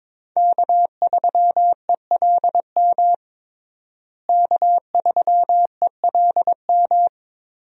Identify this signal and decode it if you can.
Morse code, heard as a keyed tone, about 22 wpm, 700 Hz. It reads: K3ELM K3ELM
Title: USA Call Signs - 5 Characters - Repeated 2x - 22wpm